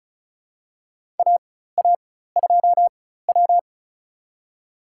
Morse code, heard as a keyed tone, about 36 wpm, spaced as for 16 wpm, 700 Hz. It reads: AA2W
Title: USA Call Signs - 4x Character Spacing - 35wpm